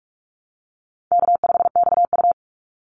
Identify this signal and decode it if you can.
Morse code, heard as a keyed tone, about 44 wpm, 700 Hz. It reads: K5XU
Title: Call Signs - 45wpm